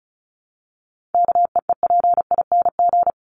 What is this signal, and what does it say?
Morse code, heard as a keyed tone, about 34 wpm, 700 Hz. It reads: KEEPING
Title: ING Suffix - Top 100 Words with Base Words 35wpm